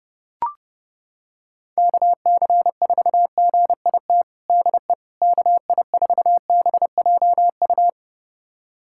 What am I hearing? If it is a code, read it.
Morse code, heard as a keyed tone, about 30 wpm, 700 Hz. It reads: KC4GIT DE KI4BJU
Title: USA Callsign DE Callsign 30wpm